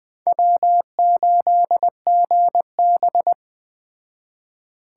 Morse code, heard as a keyed tone, about 22 wpm, 700 Hz. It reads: W8GB